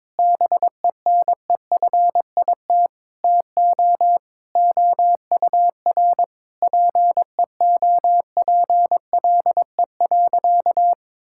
Morse code, heard as a keyed tone, about 21 wpm, 700 Hz. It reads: BENEFIT TO OUR PEOPLE.
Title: Sentences from Top 1000 Words - Review Part 6 - 22wpm